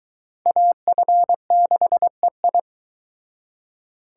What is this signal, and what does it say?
Morse code, heard as a keyed tone, about 22 wpm, 700 Hz. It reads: AF6EI